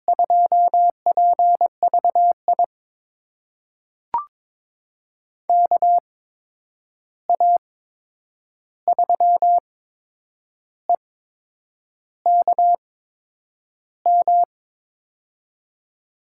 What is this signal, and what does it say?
Morse code, heard as a keyed tone, about 21 wpm, spaced as for 8 wpm, 700 Hz. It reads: 2PVI KA3EKM
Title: USA Call Signs - 6 Characters - 8x Character Spacing - 22wpm